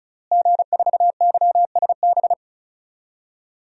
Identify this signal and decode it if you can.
Morse code, heard as a keyed tone, about 34 wpm, 700 Hz. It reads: G4YSB
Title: DXCC-100 Call Signs - 5 Characters - Repeated 3x - 35wpm